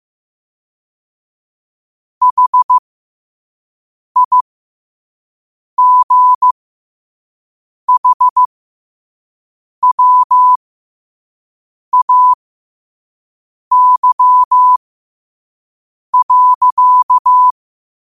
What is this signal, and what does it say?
Morse code, heard as a keyed tone, about 15 wpm, spaced as for 5 wpm, 1000 Hz. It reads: HIGHWAY.
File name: qotdfiles / 5wpm